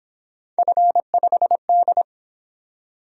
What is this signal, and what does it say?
Morse code, heard as a keyed tone, about 26 wpm, 700 Hz. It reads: F5D
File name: Speed-Racing - Three Letter-Number Combinations - 22wpm